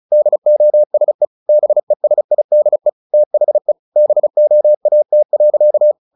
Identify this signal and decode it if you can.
Morse code, heard as a keyed tone, about 35 wpm, 600 Hz. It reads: DOSE BESIDE THE BOAT.
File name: the old man and the sea / the old man and the sea 1 35wpm